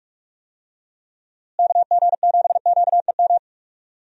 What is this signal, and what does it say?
Morse code, heard as a keyed tone, about 42 wpm, 700 Hz. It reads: KG7QEM
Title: USA Call Signs - 6 Characters - Repeated 3x - 45wpm